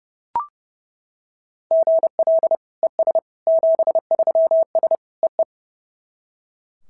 Morse code, heard as a keyed tone, about 32 wpm, 690 Hz. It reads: GL ES 73S EE
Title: SOTA QSO Practice 30wpm